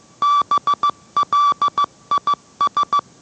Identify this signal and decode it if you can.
Morse code, heard as a keyed tone, about 15 wpm, 1160 Hz. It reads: BLIS